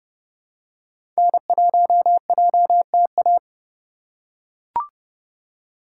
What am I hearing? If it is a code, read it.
Morse code, heard as a keyed tone, about 30 wpm, 700 Hz. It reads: N1JTA
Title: USA Call Signs - 5 Characters - 3x Character Spacing - 30wpm